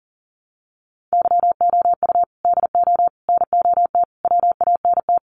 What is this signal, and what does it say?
Morse code, heard as a keyed tone, about 40 wpm, 700 Hz. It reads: YOU DO NOT WANT
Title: Sets of 4 Words 40wpm